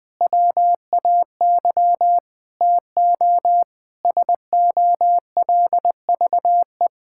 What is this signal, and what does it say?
Morse code, heard as a keyed tone, about 20 wpm, 700 Hz. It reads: WAY TO SOLVE
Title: Sentences with 4-Word Phrases - 20wpm